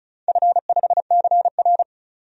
Morse code, heard as a keyed tone, about 36 wpm, 700 Hz. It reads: F5CR
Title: DXCC-100 Call Signs - 4 Characters - 35wpm